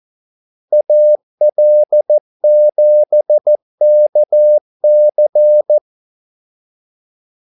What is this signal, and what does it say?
Morse code, heard as a keyed tone, about 14 wpm, 600 Hz. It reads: AL7KC
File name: Callsigns-14wpm